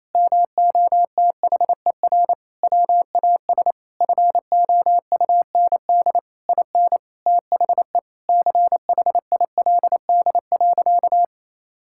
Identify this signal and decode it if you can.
Morse code, heard as a keyed tone, about 28 wpm, 700 Hz. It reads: MOTHER WAS FOUND IN THE CHILD.
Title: Sentences from Top 1000 Words - Review Part 5 - 28wpm